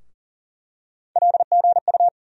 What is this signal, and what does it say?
Morse code, heard as a keyed tone, about 40 wpm, 700 Hz. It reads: LGU
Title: USA Call Sign Postfixes - 3 Letters 40wpm